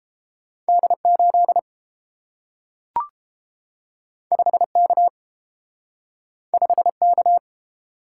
Morse code, heard as a keyed tone, about 30 wpm, 700 Hz. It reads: D 8 5 K 5 K